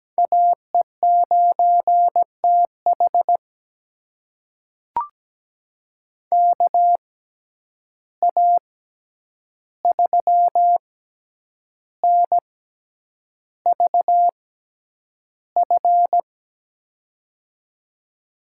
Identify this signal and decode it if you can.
Morse code, heard as a keyed tone, about 17 wpm, spaced as for 8 wpm, 700 Hz. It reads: AE9TH KA3NVF
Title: USA Call Signs - 6x Character Spacing - 17wpm